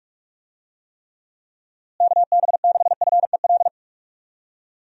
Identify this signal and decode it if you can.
Morse code, heard as a keyed tone, about 45 wpm, 700 Hz. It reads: KB6FEL
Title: USA Call Signs - 6 Characters - Repeated 2x - 45wpm